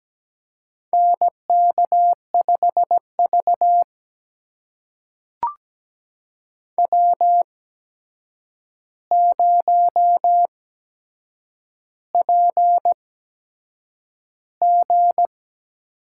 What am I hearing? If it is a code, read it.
Morse code, heard as a keyed tone, about 17 wpm, spaced as for 7 wpm, 700 Hz. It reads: NK5V W0PG
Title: USA Call Signs - 4 Characters - 8x Character Spacing - 17wpm